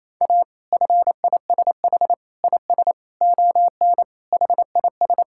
Morse code, heard as a keyed tone, about 27 wpm, 700 Hz. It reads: A FISH IS ON HIS
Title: Sentences from Top 1600 Words 28wpm